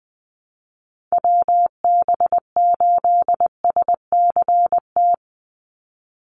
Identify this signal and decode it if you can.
Morse code, heard as a keyed tone, about 20 wpm, 700 Hz. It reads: WB8SCT
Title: Call Signs - 20wpm